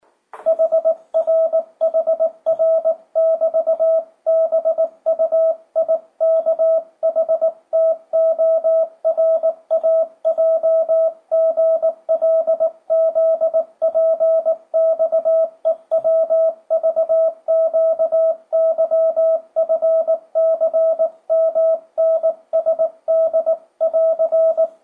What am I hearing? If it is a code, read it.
Morse code, heard as a keyed tone, about 18 wpm, 650 Hz, with radio noise. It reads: HRHR<BT>BUIKHTORAJGLZPXEWVQYFCMNSD<AR>
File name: 2025cw / speedup / 19wpm